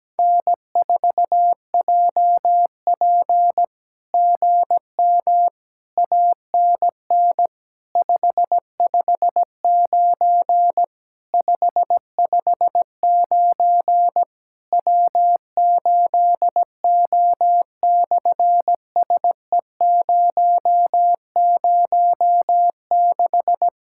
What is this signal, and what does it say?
Morse code, heard as a keyed tone, about 17 wpm, 700 Hz. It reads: N4JP GM ANN 559 559 W8O/SE006
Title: SOTA QSO Practice 17wpm